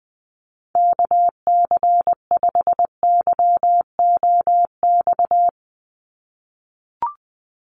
Morse code, heard as a keyed tone, about 20 wpm, 700 Hz. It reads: KC5YOX
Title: USA Call Signs 20wpm